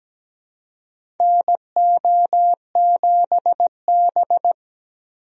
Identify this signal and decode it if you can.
Morse code, heard as a keyed tone, about 17 wpm, 700 Hz. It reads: NO7B
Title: USA Call Signs - 4 Characters - Repeated 2x - 17wpm